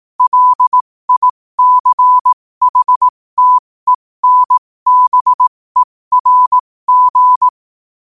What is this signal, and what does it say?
Morse code, heard as a keyed tone, about 18 wpm, spaced as for 15 wpm, 1000 Hz. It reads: LICHTENBERG
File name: qotdfiles / 15wpm